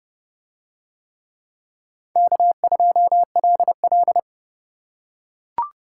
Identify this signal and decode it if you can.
Morse code, heard as a keyed tone, about 29 wpm, 700 Hz. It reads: K2LL